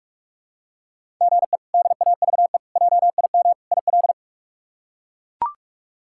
Morse code, heard as a keyed tone, about 41 wpm, 700 Hz. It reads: GE DAVE JIM IL
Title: K1USN SST QSO Practice - 45wpm